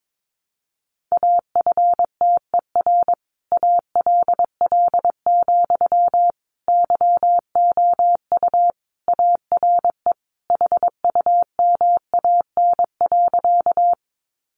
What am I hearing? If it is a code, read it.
Morse code, heard as a keyed tone, about 22 wpm, 700 Hz. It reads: AFTER ALL, YOU ARE HUMAN.
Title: Sentences from Top 600 Words 22wpm